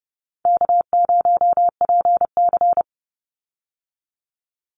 Morse code, heard as a keyed tone, about 30 wpm, 700 Hz. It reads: K0PC
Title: Easy Call Signs Repeated 3x - 30wpm